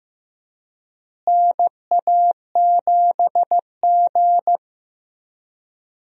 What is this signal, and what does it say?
Morse code, heard as a keyed tone, about 15 wpm, 700 Hz. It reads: NA7G
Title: USA Call Signs - 4 Characters - Repeated 3x - 15wpm